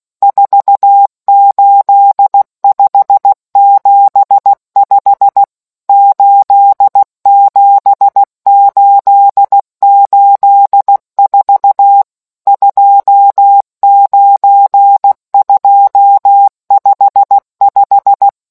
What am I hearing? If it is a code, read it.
Morse code, heard as a keyed tone, about 16 wpm, 800 Hz. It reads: 48575 87884 29255